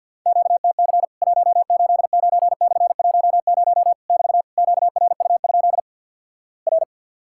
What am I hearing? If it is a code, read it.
Morse code, heard as a keyed tone, about 50 wpm, 700 Hz. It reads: QTC 189/10 <BT> QRU? R
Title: Worked All Europe DX Contest - QTC Exchange - 50wpm